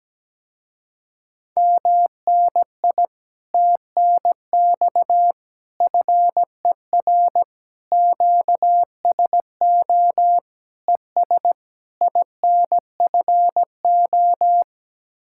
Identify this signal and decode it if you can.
Morse code, heard as a keyed tone, about 17 wpm, 700 Hz. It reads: MNI TNX FER QSO ES INFO
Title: Speed-Racing Common QSO Phrases - 17wpm